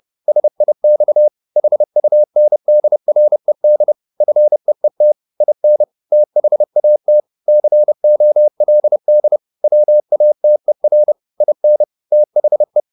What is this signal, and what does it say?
Morse code, heard as a keyed tone, about 30 wpm, 600 Hz. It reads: SIX HUNDRED FEET IN THAT COLD WATER IN THE
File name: the old man and the sea / the old man and the sea 1 30wpm